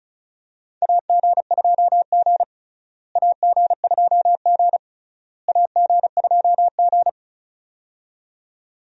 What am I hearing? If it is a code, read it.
Morse code, heard as a keyed tone, about 36 wpm, 700 Hz. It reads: AG2G AG2G AG2G
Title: USA Call Signs - 4 Characters - Repeated 3x - 35wpm